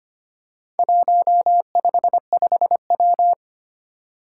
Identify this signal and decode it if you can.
Morse code, heard as a keyed tone, about 25 wpm, 700 Hz. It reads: 155W
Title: Four Challenging Letter-Number Combinations 25wpm